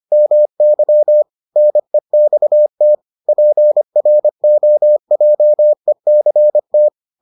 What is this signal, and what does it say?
Morse code, heard as a keyed tone, about 25 wpm, 600 Hz. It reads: MY NEXT PROJECT